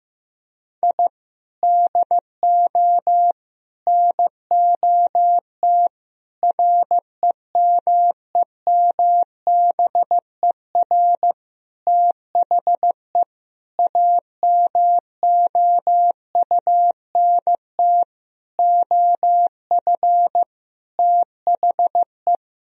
Morse code, heard as a keyed tone, about 15 wpm, 700 Hz. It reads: I DO NOT REMEMBER THE AMOUNT OF THE
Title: Sentences from Top 1000 Words - Review Part 4 - 15wpm